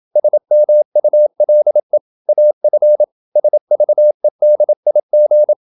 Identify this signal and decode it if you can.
Morse code, heard as a keyed tone, about 27 wpm, 600 Hz. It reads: SMULE AF SVEDIG
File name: Ferie QRQ 27wpm